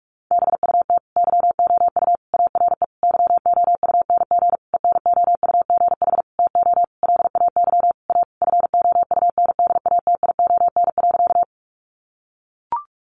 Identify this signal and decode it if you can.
Morse code, heard as a keyed tone, about 44 wpm, 700 Hz. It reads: BUT YOU ARE YOUNG ENOUGH TO LAY A FOUNDATION.
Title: Sentences from Top 1200 Words 45wpm